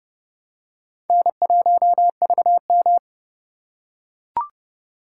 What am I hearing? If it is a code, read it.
Morse code, heard as a keyed tone, about 29 wpm, 700 Hz. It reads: N1VM
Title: USA Call Signs - 4 Characters - 2x Character Spacing - 30wpm